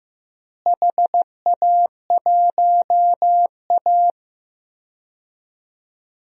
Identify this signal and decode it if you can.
Morse code, heard as a keyed tone, about 15 wpm, 700 Hz. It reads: HA1A
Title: DXCC-100 Call Signs - 4 Characters - Repeated 2x - 15wpm